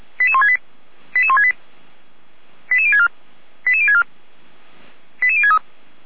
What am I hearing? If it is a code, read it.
Five-tone selective call (ZVEI), heard as quick runs of five tones, tones of about 70 ms. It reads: 88168 88168 88064 88064 88063
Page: SDR/ZVEI